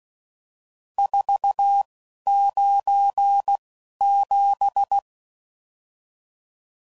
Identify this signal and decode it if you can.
Morse code, heard as a keyed tone, about 16 wpm, spaced as for 11 wpm, 800 Hz. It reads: 497
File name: CW Code Groups @10WPM/RSTs/Various RST No's